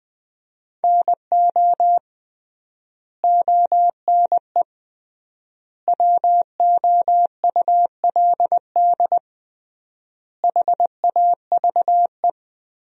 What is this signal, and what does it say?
Morse code, heard as a keyed tone, about 20 wpm, 700 Hz. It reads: NO ONE WOULD HAVE
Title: 3x Word-Spacing - Sentences from Top 300 Words - 20wpm